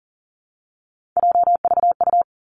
Morse code, heard as a keyed tone, about 38 wpm, 700 Hz. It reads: JVU
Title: Three Challenging Letter-Number Combinations 40wpm